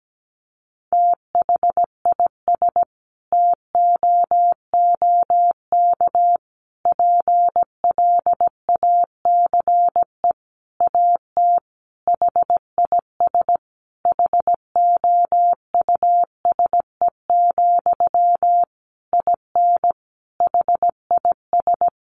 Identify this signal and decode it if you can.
Morse code, heard as a keyed tone, about 17 wpm, 700 Hz. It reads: THIS TOOK PLACE AT HIS HOUSE, IN HIS
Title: Sentences from Top 300 Words 17wpm